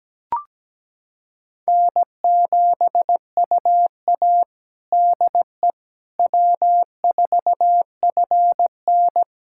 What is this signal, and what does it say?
Morse code, heard as a keyed tone, about 17 wpm, 700 Hz. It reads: N7UA DE W4FN
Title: USA Callsign DE Callsign - 4 Characters - 17wpm